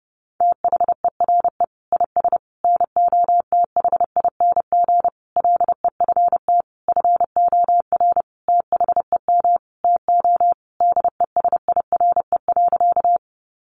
Morse code, heard as a keyed tone, about 30 wpm, 700 Hz. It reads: THERE IS NOTHING LEFT FOR THEM TO DESIRE.
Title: Sentences from Top 1100 Words 30wpm